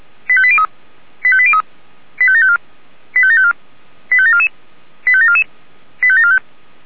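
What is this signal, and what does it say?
Five-tone selective call (ZVEI), heard as quick runs of five tones, tones of about 70 ms. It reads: 85803 85803 85754 85754 85744 85744 85745
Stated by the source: Received by SDR.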